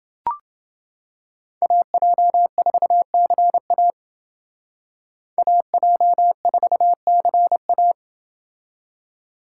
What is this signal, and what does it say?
Morse code, heard as a keyed tone, about 28 wpm, 700 Hz. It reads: AJ4CA AJ4CA